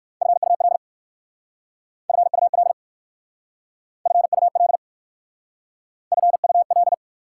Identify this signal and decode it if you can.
Morse code, heard as a keyed tone, about 55 wpm, 690 Hz. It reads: FUL FUL FUL FUL
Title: Speed-Racing Prefixes and Suffixes 45wpm